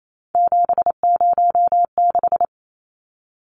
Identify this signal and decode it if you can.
Morse code, heard as a keyed tone, about 28 wpm, 700 Hz. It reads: 706
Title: Numbers - 3-Digit 28wpm